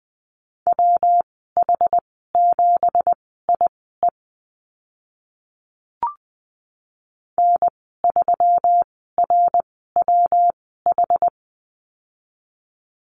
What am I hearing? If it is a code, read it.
Morse code, heard as a keyed tone, about 20 wpm, spaced as for 15 wpm, 700 Hz. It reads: WH7IE N3RWH
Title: 167 Lesson 27g - Number 7 - Call Signs 20wpm